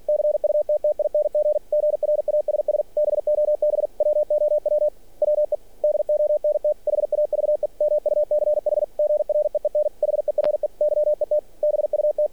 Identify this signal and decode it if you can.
Morse code, heard as a keyed tone, about 46 wpm, 600 Hz. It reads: CUTTING GRASS BOB WOW WE DONT HAVE MUCH GREEN HERE YET BUT
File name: archived / CRSnet